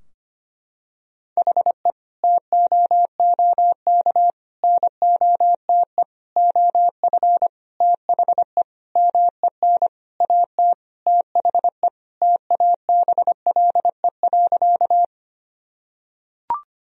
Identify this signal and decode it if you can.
Morse code, heard as a keyed tone, about 25 wpm, 700 Hz. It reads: HE TOOK NOTE OF THE MEN AT THE TABLE.
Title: Sentences from Top 1000 Words - Review Part 2 - 25wpm